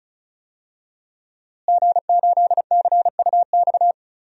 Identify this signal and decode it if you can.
Morse code, heard as a keyed tone, about 34 wpm, 700 Hz. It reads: G8CUX
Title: DXCC-100 Call Signs - 5 Characters - Repeated 2x - 35wpm